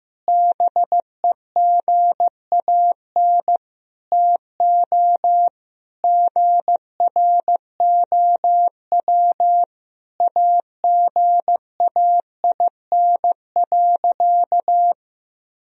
Morse code, heard as a keyed tone, about 14 wpm, 700 Hz. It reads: BEGAN TO GROW AGAIN.
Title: Sentences from Top 1000 Words - Review Part 1 - 15wpm